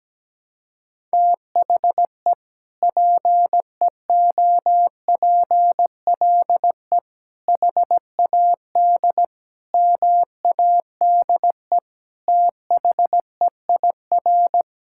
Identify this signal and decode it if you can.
Morse code, heard as a keyed tone, about 17 wpm, 700 Hz. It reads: THE PEOPLE HAD MADE THEIR
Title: Sentences from Top 1000 Words - Review Part 6 - 17wpm